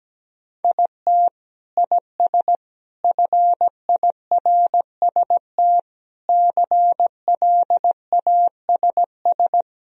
Morse code, heard as a keyed tone, about 17 wpm, 700 Hz. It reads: IT IS FIRST CLASS
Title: Sentences from Top 1000 Words - Review Part 3 - 17wpm